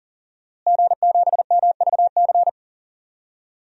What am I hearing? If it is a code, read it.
Morse code, heard as a keyed tone, about 40 wpm, 700 Hz. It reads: G7MVC
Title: DXCC-100 Call Signs - 5 Characters - Repeated 3x - 40wpm